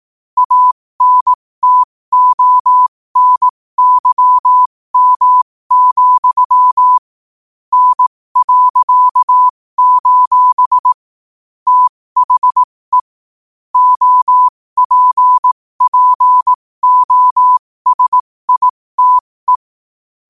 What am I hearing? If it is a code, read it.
Morse code, heard as a keyed tone, about 18 wpm, spaced as for 15 wpm, 1000 Hz. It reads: ANTONYM, N.: THE OPPOSITE
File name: qotdfiles / 15wpm